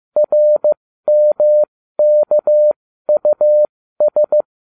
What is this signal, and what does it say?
Morse code, heard as a keyed tone, about 15 wpm, spaced as for 13 wpm, 600 Hz. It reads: RMKUS